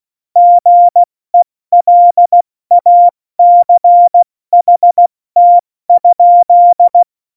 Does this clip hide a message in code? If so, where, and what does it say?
Yes: Morse code from 0.4 to 7.0 s, heard as a keyed tone, about 16 wpm, 700 Hz. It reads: GELACHT?